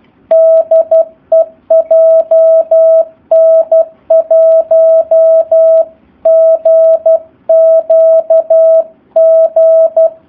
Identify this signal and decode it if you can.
Morse code, heard as a keyed tone, about 12 wpm, 650 Hz. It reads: DE JN1GQG
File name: jh8cbh / 2019cw / SP04- w